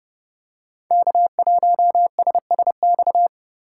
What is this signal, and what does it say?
Morse code, heard as a keyed tone, about 28 wpm, 700 Hz. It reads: K1SSX
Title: Code Groups - 5 Letter:Numbers - 30wpm